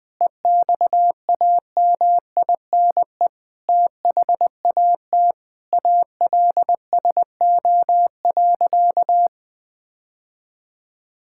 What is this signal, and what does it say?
Morse code, heard as a keyed tone, about 20 wpm, 700 Hz. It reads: EXAMINE THAT ALSO.
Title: Sentences from Top 1600 Words 20wpm